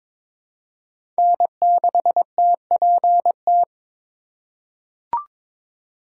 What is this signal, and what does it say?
Morse code, heard as a keyed tone, about 22 wpm, 700 Hz. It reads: N6TPT
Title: USA Call Signs - 6x Character Spacing - 22wpm